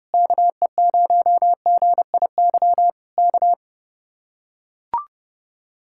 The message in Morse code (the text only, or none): KE0GIY K